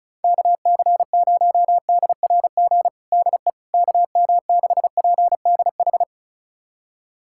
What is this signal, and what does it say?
Morse code, heard as a keyed tone, about 36 wpm, 700 Hz. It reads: KC0DRG DE KM6PDH
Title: USA Callsign DE Callsign - 6 Characters - 35wpm